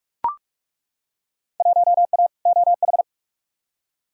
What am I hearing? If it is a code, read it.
Morse code, heard as a keyed tone, about 42 wpm, 700 Hz. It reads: 1A OH
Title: Field Day Exchanges - 45wpm